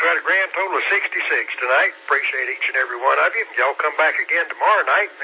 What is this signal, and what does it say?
Emergency Alert System (SAME) tones, heard as harsh buzzing bursts, with speech. It is heard from 0.9 to 4.5 s.